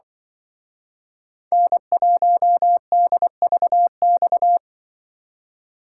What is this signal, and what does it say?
Morse code, heard as a keyed tone, about 23 wpm, 700 Hz. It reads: N1DVX
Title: Speed Racing - USA 1x3 Call Signs 20wpm